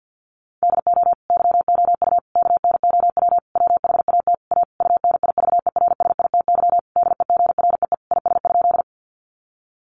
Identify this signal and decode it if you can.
Morse code, heard as a keyed tone, about 50 wpm, 700 Hz. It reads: DO YOU KNOW WHAT A UNIVERSITY DEGREE IS?
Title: Sentences from Top 1500 Words 50wpm